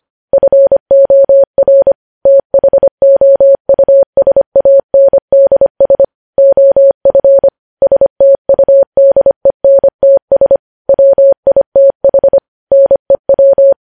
Morse code, heard as a keyed tone, about 25 wpm, 550 Hz. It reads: FOR THOUSANDS OF STUDENTS WITH NEW
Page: news headlines in morse code at 25 wpm